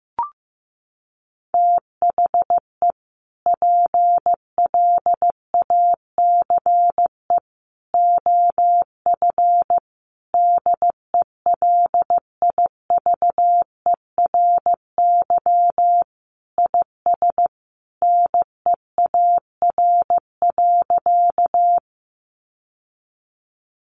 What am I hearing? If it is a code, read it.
Morse code, heard as a keyed tone, about 15 wpm, 700 Hz. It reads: THE PLACE OF DELIVERY IS NEAR.
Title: Sentences from Top 1500 Words 15wpm